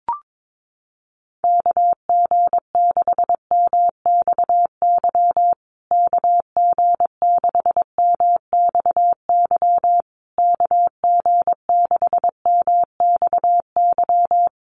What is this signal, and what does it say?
Morse code, heard as a keyed tone, about 22 wpm, 700 Hz. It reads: KG6MXY KG6MXY KG6MXY
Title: USA Call Signs Repeated 3x 22wpm